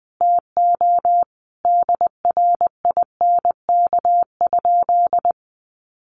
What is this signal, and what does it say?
Morse code, heard as a keyed tone, about 20 wpm, 700 Hz. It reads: TO DRINK?
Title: Sentences from Top 1500 Words 20wpm